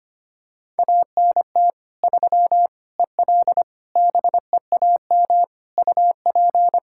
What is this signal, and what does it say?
Morse code, heard as a keyed tone, about 25 wpm, 700 Hz. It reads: ANT 3 EL BEAM UP
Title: Practice QSO - Exchange 2 - Info - 25wpm